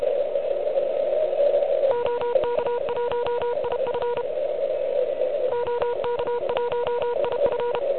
Morse code, heard as a keyed tone, about 32 wpm, 970 Hz, with noise. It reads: OK1IF OK1IF
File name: ARCHIV / QCX / QCX Pileup